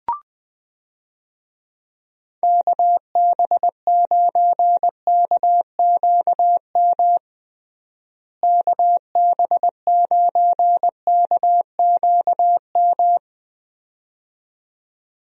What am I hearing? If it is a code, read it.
Morse code, heard as a keyed tone, about 20 wpm, 700 Hz. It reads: KB9KQM KB9KQM
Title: USA Call Signs - 6 Characters - Repeated 2x - 20wpm